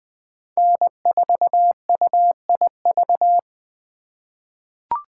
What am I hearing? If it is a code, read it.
Morse code, heard as a keyed tone, about 20 wpm, 700 Hz. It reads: N4UIV
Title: USA Call Signs - 5 Characters - 3x Character Spacing - 20wpm